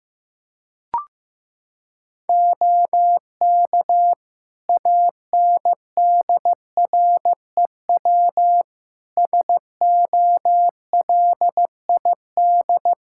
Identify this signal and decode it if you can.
Morse code, heard as a keyed tone, about 15 wpm, 700 Hz. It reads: OK ANDREW SOLID
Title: Practice QSO - Exchange 3 - More Info - 15wpm